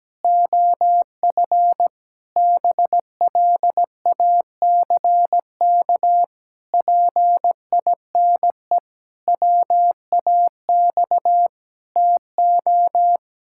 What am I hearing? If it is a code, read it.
Morse code, heard as a keyed tone, about 17 wpm, 700 Hz. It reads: OF BLACK PINE WAX TO
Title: Pangram Challenge v2 - 17wpm